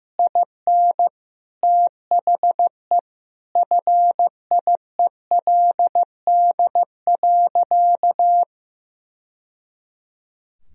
Morse code, heard as a keyed tone, about 16 wpm, 700 Hz. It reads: IN THE FIELD.
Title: Sentences from Top 1000 Words - Review Part 2 - 15wpm